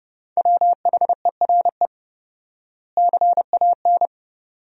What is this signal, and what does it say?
Morse code, heard as a keyed tone, about 28 wpm, 700 Hz. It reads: WHERE CAN
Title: 4x Word-Spacing - Sentences from Top 500 Words - 30wpm